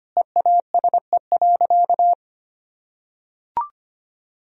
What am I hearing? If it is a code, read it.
Morse code, heard as a keyed tone, about 25 wpm, 700 Hz. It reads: EASE.
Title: Sentences from Top 1000 Words - Review Part 1 - 25wpm